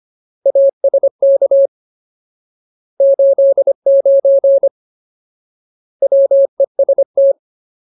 Morse code, heard as a keyed tone, about 25 wpm, 550 Hz. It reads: ASK 89 WEST
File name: Ultimate Word List / 25WPM Ultimate